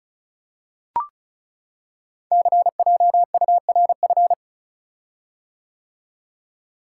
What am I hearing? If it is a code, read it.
Morse code, heard as a keyed tone, about 36 wpm, 700 Hz. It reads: CJURF
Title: Code Groups - 5 Letters - 35wpm